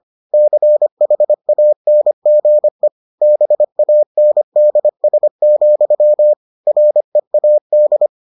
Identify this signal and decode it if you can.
Morse code, heard as a keyed tone, about 25 wpm, 600 Hz. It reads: CHANGE BANDS, READ